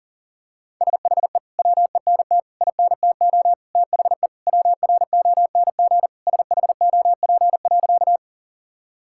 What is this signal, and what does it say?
Morse code, heard as a keyed tone, about 40 wpm, 700 Hz. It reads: SHE WENT INTO THE WRONG SHOP.
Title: Sentences from Top 1600 Words 40wpm